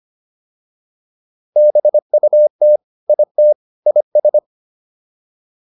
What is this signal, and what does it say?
Morse code, heard as a keyed tone, about 25 wpm, 600 Hz. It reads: BUT IT IS